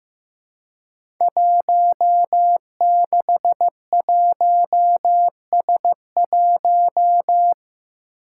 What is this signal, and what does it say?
Morse code, heard as a keyed tone, about 15 wpm, 700 Hz. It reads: 161S1
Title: Five Challenging Letter-Number Combinations 15wpm